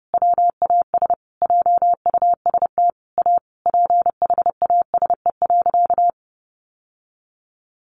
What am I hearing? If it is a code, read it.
Morse code, heard as a keyed tone, about 30 wpm, 700 Hz. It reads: WAS JUST A PHASE.
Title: Sentences from Top 1200 Words 30wpm